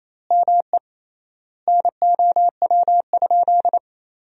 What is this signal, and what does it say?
Morse code, heard as a keyed tone, about 28 wpm, 700 Hz. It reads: ME NOW?
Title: 3x Word-Spacing - Sentences from Top 400 Words - 28wpm